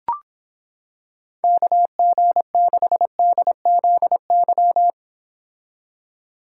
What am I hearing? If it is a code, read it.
Morse code, heard as a keyed tone, about 26 wpm, 700 Hz. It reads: KG6DZY